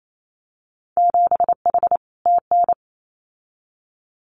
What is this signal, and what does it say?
Morse code, heard as a keyed tone, about 28 wpm, 700 Hz. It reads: 7H TN
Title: Winter Field Day Exchanges Encore 28wpm